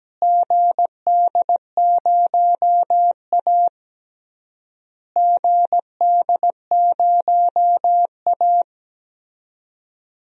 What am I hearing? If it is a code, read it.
Morse code, heard as a keyed tone, about 17 wpm, 700 Hz. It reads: GD0A GD0A
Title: DXCC-100 Call Signs - 4 Characters - Repeated 3x - 17wpm